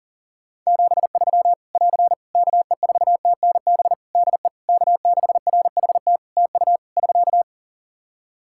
Morse code, heard as a keyed tone, about 40 wpm, 700 Hz. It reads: 73 <AR> KE4TNB DE K6RHT TU <SK>
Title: Practice QSO - Exchange 4 - Ending - 40wpm